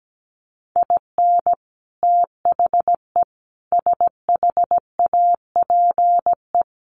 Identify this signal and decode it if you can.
Morse code, heard as a keyed tone, about 17 wpm, 700 Hz. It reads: IN THE SHAPE
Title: Sets of 5 Words Encore 17wpm